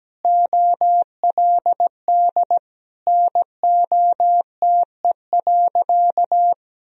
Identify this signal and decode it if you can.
Morse code, heard as a keyed tone, about 17 wpm, 700 Hz. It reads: OLD NOTE.
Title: Sentences from Top 1000 Words - Review Part 2 - 17wpm